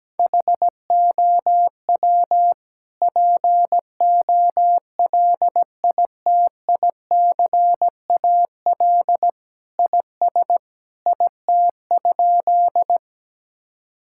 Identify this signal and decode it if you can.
Morse code, heard as a keyed tone, about 17 wpm, 700 Hz. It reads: HOW POLITICAL IS IT?
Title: Sentences from Top 1000 Words - Review Part 5 - 17wpm